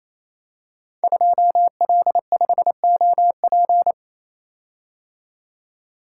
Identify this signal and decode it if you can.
Morse code, heard as a keyed tone, about 28 wpm, 700 Hz. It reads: 2L5OP
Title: Code Groups - 5 Letter:Numbers - 28wpm